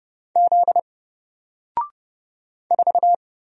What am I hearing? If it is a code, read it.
Morse code, heard as a keyed tone, about 30 wpm, 700 Hz. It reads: Z 4
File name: Single-Letter-Number-30wpm